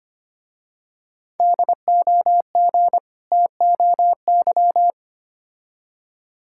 Morse code, heard as a keyed tone, about 25 wpm, 700 Hz. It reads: DOG TOY
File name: practice / 25wpm / Difficult Words with Hints - 25wpm